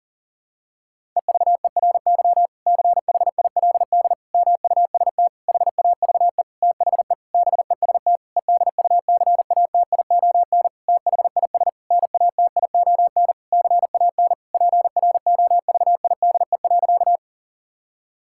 Morse code, heard as a keyed tone, about 40 wpm, 700 Hz. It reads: EVERY CHILD MUST HAVE THE BEST EDUCATION THIS NATION CAN PROVIDE.
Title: Sentences from Top 1000 Words - Review Part 6 - 40wpm